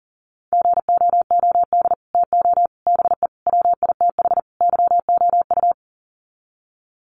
Morse code, heard as a keyed tone, about 38 wpm, 700 Hz. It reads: GOOD TO BE WITH YOU
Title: Sets of 5 Words Encore2 40wpm